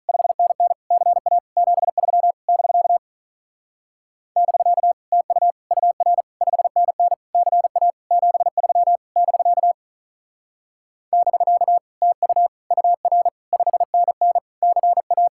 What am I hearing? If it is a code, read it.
Morse code, heard as a keyed tone, about 46 wpm, 700 Hz. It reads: 5NN CA 73 <BK> <BK> TU UR 5NN CA 73 <BK> <BK> TU UR 5NN CA